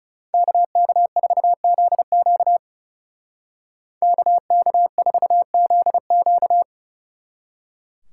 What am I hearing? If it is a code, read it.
Morse code, heard as a keyed tone, about 36 wpm, 700 Hz. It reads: KK4ZQ KK4ZQ
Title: Speed Racing - USA Call Signs 30wpm